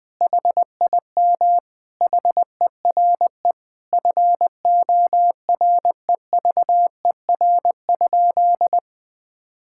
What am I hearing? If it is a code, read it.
Morse code, heard as a keyed tone, about 20 wpm, 700 Hz. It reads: HIM HERE FOREVER?
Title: Sentences from Top 1600 Words 20wpm